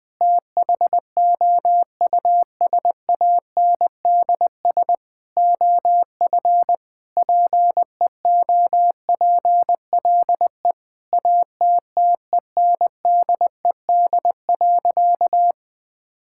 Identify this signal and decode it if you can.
Morse code, heard as a keyed tone, about 20 wpm, 700 Hz. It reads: THOUSANDS OF PEOPLE ATTENDED.
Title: Sentences with 4-Word Phrases - Spoken Beginning - 20wpm